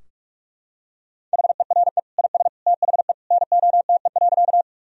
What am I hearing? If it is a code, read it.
Morse code, heard as a keyed tone, about 41 wpm, 700 Hz. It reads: HERE IS THE NOTE.
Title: Sentences from Top 1000 Words - Review Part 2 - 45wpm